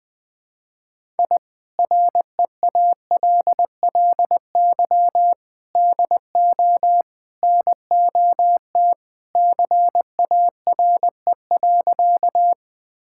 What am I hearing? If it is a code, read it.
Morse code, heard as a keyed tone, about 21 wpm, 700 Hz. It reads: I REALLY DO NOT CARE.
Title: Sentences with 4-Word Phrases - 20wpm